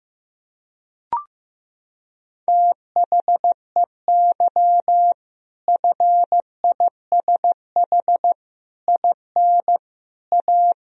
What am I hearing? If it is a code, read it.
Morse code, heard as a keyed tone, about 16 wpm, 700 Hz. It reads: THEY FISH IN A
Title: Sentences from Top 1000 Words - Review Part 1 - 15wpm